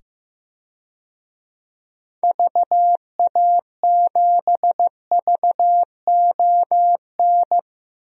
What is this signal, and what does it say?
Morse code, heard as a keyed tone, about 15 wpm, 690 Hz. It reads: VA7VON
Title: DXCC-100 Call Signs - 6 Characters - 15wpm